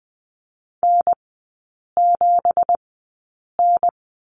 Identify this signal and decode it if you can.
Morse code, heard as a keyed tone, about 20 wpm, spaced as for 8 wpm, 690 Hz. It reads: N7N
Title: 166 Lesson 27e - Number 7 - Three Characters 20wpm